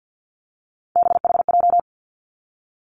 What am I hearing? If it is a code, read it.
Morse code, heard as a keyed tone, about 50 wpm, 700 Hz. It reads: BHJ
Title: Three Challenging Letter-Number Combinations 50wpm